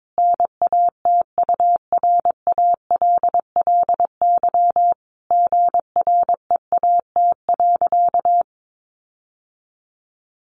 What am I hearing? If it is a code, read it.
Morse code, heard as a keyed tone, about 22 wpm, 700 Hz. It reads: NATURALLY GREAT.
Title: Sentences from Top 1300 Words 22wpm